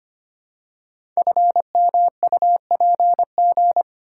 Morse code, heard as a keyed tone, about 25 wpm, 700 Hz. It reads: FMUPG
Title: Code Groups - 5 Letters - 25wpm